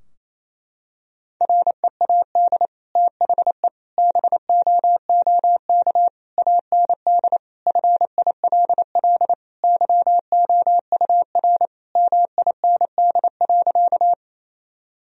Morse code, heard as a keyed tone, about 28 wpm, 700 Hz. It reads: READ THE BOOK AND FILL YOUR MIND.
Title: Sentences from Top 1000 Words - Review Part 3 - 28wpm